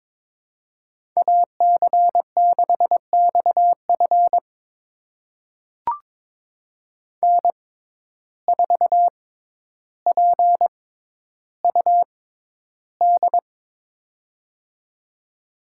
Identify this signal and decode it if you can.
Morse code, heard as a keyed tone, about 22 wpm, spaced as for 11 wpm, 700 Hz. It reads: AC6XF N4PUD
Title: USA Call Signs - 5 Characters - 6x Character Spacing - 22wpm